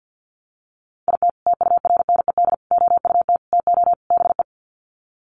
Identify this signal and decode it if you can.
Morse code, heard as a keyed tone, about 48 wpm, 700 Hz. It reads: IT TURNED OUT TO BE
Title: Sets of 5 Words 50wpm